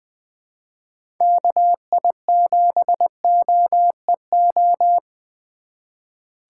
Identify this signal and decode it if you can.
Morse code, heard as a keyed tone, about 20 wpm, 700 Hz. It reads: KI7OEO
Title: USA Call Signs - 6 Characters - 4x Character Spacing - 20wpm